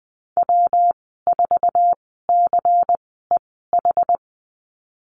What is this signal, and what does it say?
Morse code, heard as a keyed tone, about 20 wpm, spaced as for 14 wpm, 700 Hz. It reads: W4CEH
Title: 140 Lesson 23g - Letter f - Call Signs 20wpm